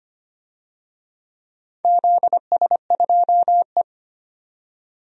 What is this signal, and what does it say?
Morse code, heard as a keyed tone, about 25 wpm, 700 Hz. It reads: ZS2E
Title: DXCC-100 Call Signs - 4 Characters - 25wpm